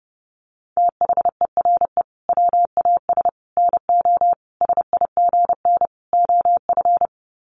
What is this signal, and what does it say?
Morse code, heard as a keyed tone, about 30 wpm, 700 Hz. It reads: THERE WAS NO SIGN OF
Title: Sets of 5 Words Encore2 30wpm